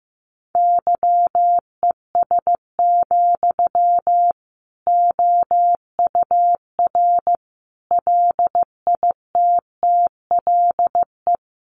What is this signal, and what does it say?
Morse code, heard as a keyed tone, about 15 wpm, 700 Hz. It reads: YES, OUR LITTLE
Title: Sentences from Top 700 Words 15wpm